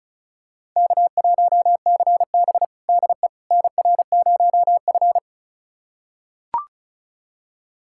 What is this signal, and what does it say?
Morse code, heard as a keyed tone, about 34 wpm, 700 Hz. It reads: K1CB DE NR0F
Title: USA Callsign DE Callsign - 4 Characters - 35wpm